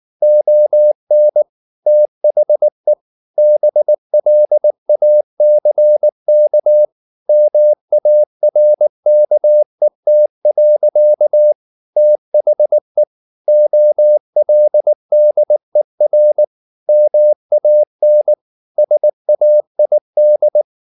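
Morse code, heard as a keyed tone, about 19 wpm, 600 Hz. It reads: ON THE BLACK MARKET. THE OLDER MAN SAID
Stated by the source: Morse code at 19 wpm from CW med Gnister, reading Ken Follett's novel Never in English.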